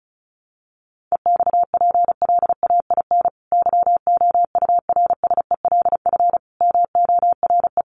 Morse code, heard as a keyed tone, about 34 wpm, 700 Hz. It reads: EXPLAIN YOURSELF MORE
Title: Sentences from Top 1300 Words 35wpm